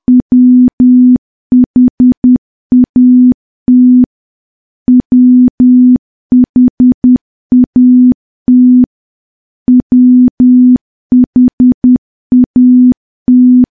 Morse code, heard as a keyed tone, about 10 wpm, 250 Hz. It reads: WHAT WHAT WHAT
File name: morse / morse-training-10WPM